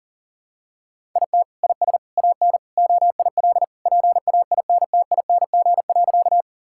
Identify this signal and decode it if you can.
Morse code, heard as a keyed tone, about 40 wpm, 700 Hz. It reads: IT IS AN OIL PAINTING.
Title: Sentences from Top 1000 Words - Review Part 6 - 40wpm